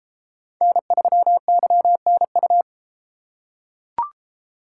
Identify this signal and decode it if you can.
Morse code, heard as a keyed tone, about 31 wpm, 700 Hz. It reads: N3YNU